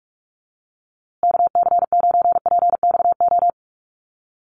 Morse code, heard as a keyed tone, about 44 wpm, 700 Hz. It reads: KC9PXO
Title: USA Call Signs Repeated 3x 45wpm